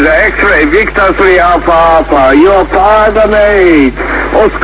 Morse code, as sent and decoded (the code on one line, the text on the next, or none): none